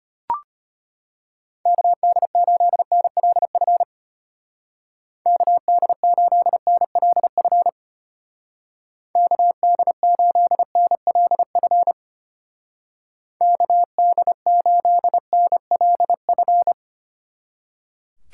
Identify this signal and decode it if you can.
Morse code, heard as a keyed tone, about 33 wpm, 700 Hz. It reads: KD8NLF KD8NLF KD8NLF KD8NLF